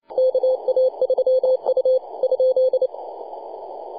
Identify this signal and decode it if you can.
Morse code, heard as a keyed tone, about 29 wpm, 520 Hz. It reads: KA3U ?
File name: posts / 2015-june-21-0848z-yb4ir-9-14-010mhz-cw